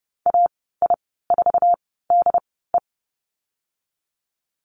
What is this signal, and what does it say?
Morse code, heard as a keyed tone, about 30 wpm, spaced as for 17 wpm, 700 Hz. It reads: AI4DE
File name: Lesson 14g - Number 2 - Call Signs 30wpm